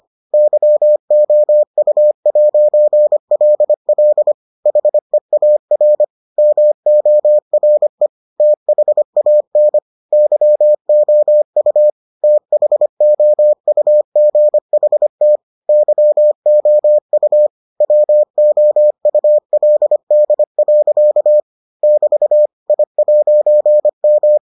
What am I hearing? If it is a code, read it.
Morse code, heard as a keyed tone, about 25 wpm, 600 Hz. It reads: YOU'LL HEAR MORE THAN YOU THOUGHT YOU WOULD. <BT> I'M